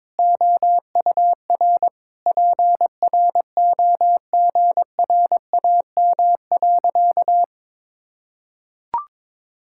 Morse code, heard as a keyed tone, about 22 wpm, 700 Hz. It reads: OUR PROGRAM.
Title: Sentences from Top 1000 Words - Review Part 5 - 22wpm